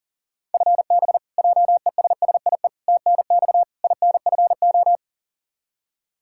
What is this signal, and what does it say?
Morse code, heard as a keyed tone, about 40 wpm, 700 Hz. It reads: FB JESSIE TNX INFO
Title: Practice QSO - Exchange 3 - More Info - 40wpm